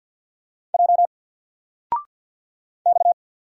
Morse code, heard as a keyed tone, about 48 wpm, 700 Hz. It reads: J X
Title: Single Character (Letter:Number:Punctuation:Prosign) Alt 50wpm